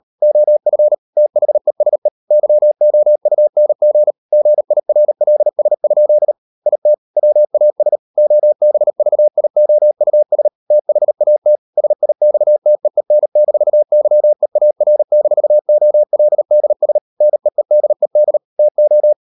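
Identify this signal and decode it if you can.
Morse code, heard as a keyed tone, about 38 wpm, 600 Hz. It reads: OF THESE YOUNG GIRLS? IT WAS OBVIOUS THAT SIXTEEN-YEAR-OLDS NEEDED TO
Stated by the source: Morse code practice at 38 WPM, sending Ken Follett's novel Never in English, from CW med Gnister.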